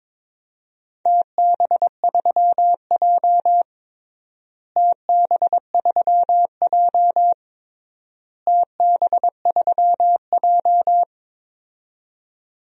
Morse code, heard as a keyed tone, about 22 wpm, 700 Hz. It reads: TB3J TB3J TB3J
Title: DXCC-100 Call Signs - 4 Characters - Repeated 3x - 22wpm